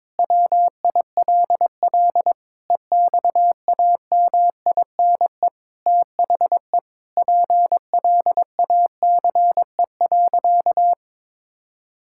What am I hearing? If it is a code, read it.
Morse code, heard as a keyed tone, about 22 wpm, 700 Hz. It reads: WILL EXAMINE THE PLACE.
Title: Sentences from Top 1600 Words 22wpm